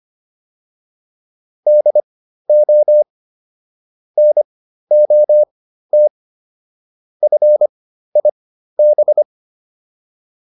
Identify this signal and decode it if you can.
Morse code, heard as a keyed tone, about 25 wpm, spaced as for 13 wpm, 600 Hz. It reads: DO NOT FIB